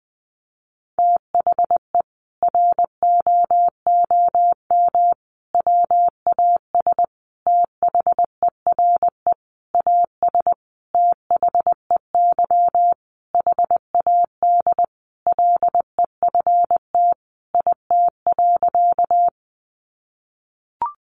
Morse code, heard as a keyed tone, about 20 wpm, 700 Hz. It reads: THE ROOM WAS THERE AS THEY HAD LEFT IT.
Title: A Sound Of Thunder 20wpm